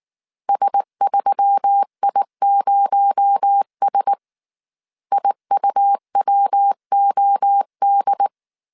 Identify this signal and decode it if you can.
Morse code, heard as a keyed tone, about 19 wpm, 800 Hz. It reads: S3I0S IUWOD
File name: CW / DET / 1 6 1500C S A 18wpm 2255